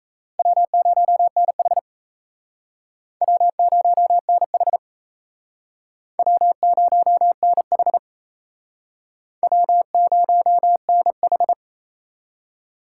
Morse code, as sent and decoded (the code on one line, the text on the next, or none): .-- ----- -. .... / .-- ----- -. .... / .-- ----- -. .... / .-- ----- -. ....
W0NH W0NH W0NH W0NH